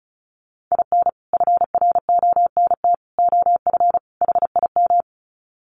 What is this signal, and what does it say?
Morse code, heard as a keyed tone, about 34 wpm, 700 Hz. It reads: IN FRONT OF HIM
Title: Sets of 4 Words 35wpm